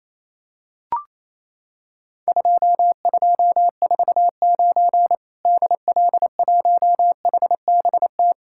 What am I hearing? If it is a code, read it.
Morse code, heard as a keyed tone, about 28 wpm, 700 Hz. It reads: 2249 DL1HBT
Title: Worked All Europe DX Contest - QTC Exchange - 28wpm